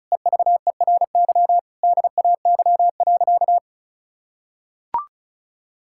Morse code, heard as a keyed tone, about 34 wpm, 700 Hz. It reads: EVERY DAY.
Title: Sentences from Top 1600 Words 35wpm